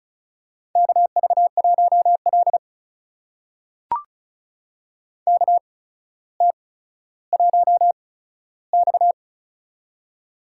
Morse code, heard as a keyed tone, about 36 wpm, spaced as for 15 wpm, 700 Hz. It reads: KV1L KT1X
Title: USA Call Signs - 4 Characters - 8x Character Spacing - 35wpm